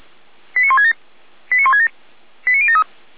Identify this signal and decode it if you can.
Five-tone selective call (ZVEI), heard as quick runs of five tones, tones of about 70 ms. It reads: 89167 89167 89073